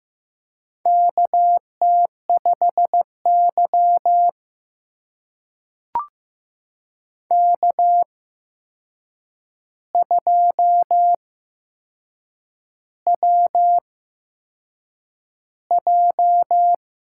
Morse code, heard as a keyed tone, about 15 wpm, spaced as for 6 wpm, 700 Hz. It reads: KT5Y K2WJ
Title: USA Call Signs - 4 Characters - 8x Character Spacing - 15wpm